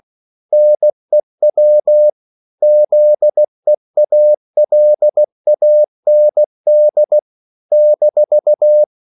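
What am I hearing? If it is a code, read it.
Morse code, heard as a keyed tone, about 16 wpm, 600 Hz. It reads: NEW ZEALAND -
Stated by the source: CW practice of national capitals at 16 words per minute, from ON6ZQ.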